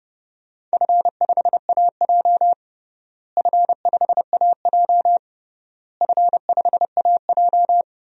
Morse code, heard as a keyed tone, about 31 wpm, 700 Hz. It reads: F5AJ F5AJ F5AJ
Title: DXCC-100 Call Signs - 4 Characters - Repeated 3x - 30wpm